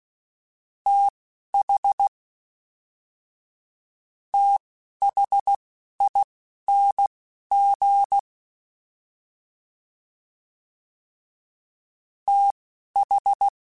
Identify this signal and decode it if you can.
Morse code, heard as a keyed tone, about 16 wpm, spaced as for 12 wpm, 800 Hz. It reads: TH THING TH
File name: THs / TH / TH Prefixes 1x1 @10